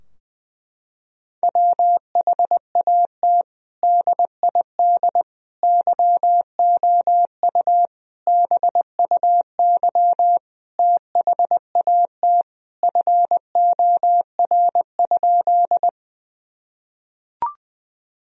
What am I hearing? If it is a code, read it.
Morse code, heard as a keyed tone, about 19 wpm, 700 Hz. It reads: WHAT DID YOU BUY THAT FOR?
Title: Sentences from Top 1000 Words - Review Part 4 - 20wpm